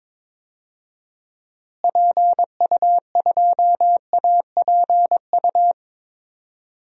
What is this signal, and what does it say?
Morse code, heard as a keyed tone, about 22 wpm, 700 Hz. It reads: PU2APU
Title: DXCC-100 Call Signs - 6 Characters - Repeated 2x - 22wpm